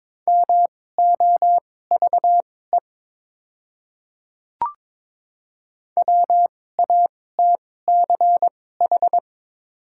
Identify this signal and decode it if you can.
Morse code, heard as a keyed tone, about 21 wpm, spaced as for 16 wpm, 700 Hz. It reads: MOVE WATCH
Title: 2x Character Spacing - Top 300 Words - 22wpm